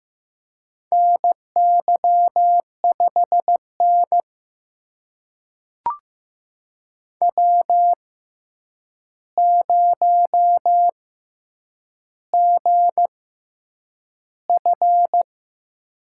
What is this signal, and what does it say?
Morse code, heard as a keyed tone, about 15 wpm, spaced as for 8 wpm, 700 Hz. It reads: NY5N W0GF
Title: USA Call Signs - 4 Characters - 6x Character Spacing - 15wpm